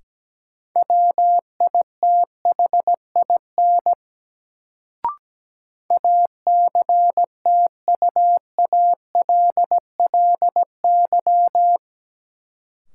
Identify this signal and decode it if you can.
Morse code, heard as a keyed tone, about 17 wpm, 700 Hz. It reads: WITHIN ACTUALLY
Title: Sentences from Top 700 Words - New Word Drill - 17wpm